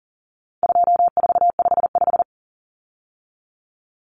Morse code, heard as a keyed tone, about 37 wpm, 700 Hz. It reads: 2455
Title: Common Cut Numbers - 4-Digit 40wpm